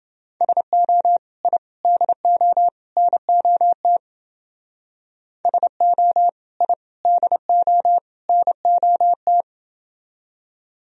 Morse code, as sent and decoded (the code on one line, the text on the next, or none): ... --- / .. / -.. --- / -. --- - / ... --- / .. / -.. --- / -. --- -
SO I DO NOT SO I DO NOT